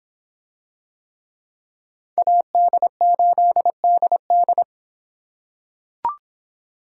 Morse code, heard as a keyed tone, about 26 wpm, 700 Hz. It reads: AD8DD